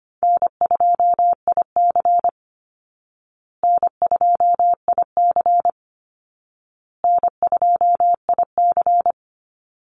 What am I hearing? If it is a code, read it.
Morse code, heard as a keyed tone, about 25 wpm, 700 Hz. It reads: N2IC N2IC N2IC
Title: Easy Call Signs Repeated 3x - 25wpm